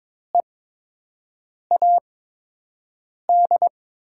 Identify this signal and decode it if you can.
Morse code, heard as a keyed tone, about 21 wpm, spaced as for 6 wpm, 700 Hz. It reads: EAD
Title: 8x Character Spacing - Top 300 Words - 22wpm